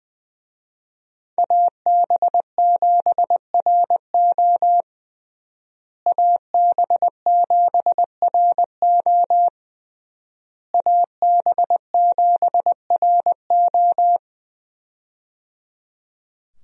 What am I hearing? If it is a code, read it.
Morse code, heard as a keyed tone, about 20 wpm, 700 Hz. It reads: AB7RO AB7RO AB7RO
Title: USA Call Signs - 5 Characters - Repeated 3x - 20wpm